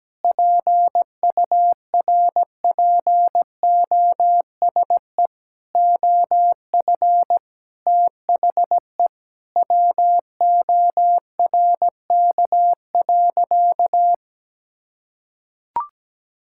Morse code, heard as a keyed tone, about 17 wpm, 700 Hz. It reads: PURPOSE OF THE WORK.
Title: Sentences from Top 1000 Words - Review Part 6 - 17wpm